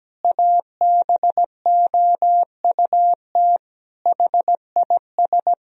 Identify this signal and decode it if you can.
Morse code, heard as a keyed tone, about 17 wpm, 700 Hz. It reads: ABOUT HIS
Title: Sentences from Top 1000 Words - Review Part 3 - 17wpm